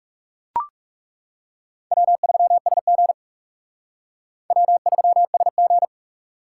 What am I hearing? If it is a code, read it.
Morse code, heard as a keyed tone, about 37 wpm, 700 Hz. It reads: W3SG W3SG
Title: Speed Racing - USA 1x2 Call Signs 30wpm